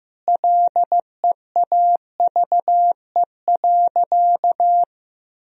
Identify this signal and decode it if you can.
Morse code, heard as a keyed tone, about 15 wpm, 700 Hz. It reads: LEAVE.
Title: Sentences from Top 1000 Words - Review Part 2 - 15wpm